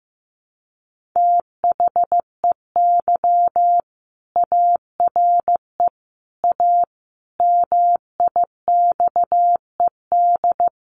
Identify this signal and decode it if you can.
Morse code, heard as a keyed tone, about 15 wpm, 700 Hz. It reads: THEY ARE A MIXED
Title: Sentences from Top 1500 Words 15wpm